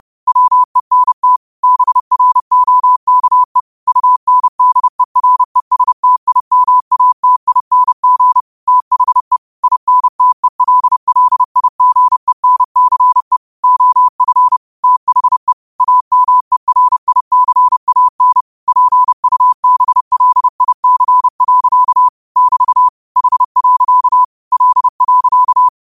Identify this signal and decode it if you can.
Morse code, heard as a keyed tone, about 30 wpm, 1000 Hz. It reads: WENT BROKE UNDERESTIMATING THE INTELLIGENCE OF THE AMERICAN PUBLIC. <BT> H. L.
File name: qotdfiles / 30wpm